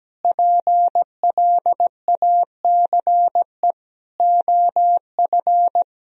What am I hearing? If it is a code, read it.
Morse code, heard as a keyed tone, about 17 wpm, 700 Hz. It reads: PLACE OF
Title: Sentences from Top 1000 Words - Review Part 4 - 17wpm